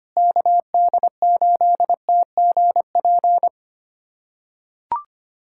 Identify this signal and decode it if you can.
Morse code, heard as a keyed tone, about 25 wpm, 700 Hz. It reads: KD8TGP
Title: USA Callsign DE Callsign - 6 Characters - 25wpm